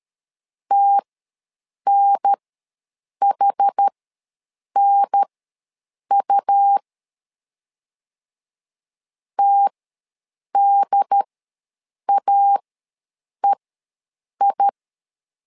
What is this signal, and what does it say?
Morse code, heard as a keyed tone, about 13 wpm, spaced as for 7 wpm, 800 Hz. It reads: TNHNU TDAEI
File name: LEZIONI / 2 UHDN / 2 UHDN ws 12wpm 1708